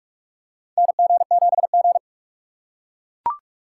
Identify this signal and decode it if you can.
Morse code, heard as a keyed tone, about 41 wpm, 700 Hz. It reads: NG7G
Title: USA Call Signs - 4 Characters - 4x Character Spacing - 45wpm